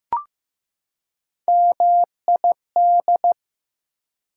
Morse code, heard as a keyed tone, about 15 wpm, 700 Hz. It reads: MID
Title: Prefixes and Suffixes 15wpm